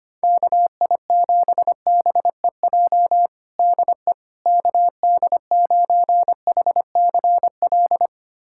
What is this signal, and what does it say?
Morse code, heard as a keyed tone, about 25 wpm, 700 Hz. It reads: KI7BEJ DE KD9HCL
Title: USA Callsign DE Callsign - 6 Characters - 25wpm